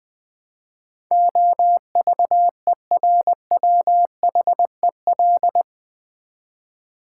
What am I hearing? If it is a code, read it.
Morse code, heard as a keyed tone, about 20 wpm, 700 Hz. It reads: OVERWHEL
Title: Build A Word - 4 Letters At A Time - 12-Letter Words - 20wpm